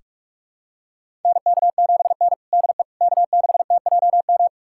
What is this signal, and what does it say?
Morse code, heard as a keyed tone, about 46 wpm, 700 Hz. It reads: NK7N DE K6TJM
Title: USA Callsign DE Callsign 45wpm